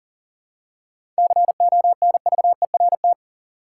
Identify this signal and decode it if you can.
Morse code, heard as a keyed tone, about 40 wpm, 700 Hz. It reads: CONVERT
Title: Seven Letter Words - 40wpm